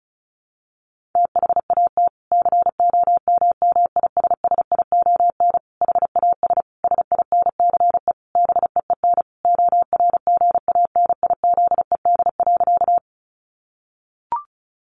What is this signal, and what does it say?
Morse code, heard as a keyed tone, about 34 wpm, 700 Hz. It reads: THAT COMMISSION HAS SINCE BEEN ORGANIZED.
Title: Sentences from Top 1400 Words 35wpm